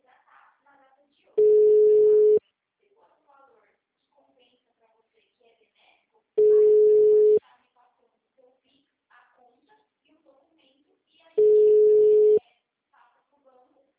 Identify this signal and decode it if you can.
Ringback tone (Europe, 425 Hz), with speech.